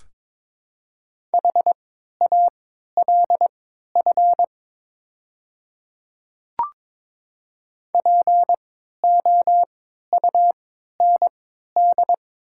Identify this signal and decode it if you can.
Morse code, heard as a keyed tone, about 22 wpm, spaced as for 12 wpm, 700 Hz. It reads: HALF POUND
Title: 3x Character Spacing - Top 500 Words - 22wpm